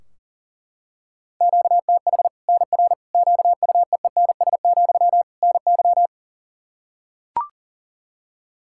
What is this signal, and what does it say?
Morse code, heard as a keyed tone, about 40 wpm, 700 Hz. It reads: QTH NR QUEENS, NY
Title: Practice QSO - Exchange 1 - Beginning - 40wpm